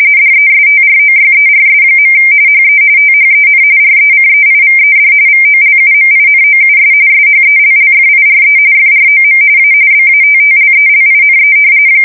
RTTY, about 45.45 baud. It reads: DESTROYERS. HE THEN RECEIVED VOLUMINOUS SHELLFIRE FROM A BATTLELINE
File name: fsk170 / battleshipnj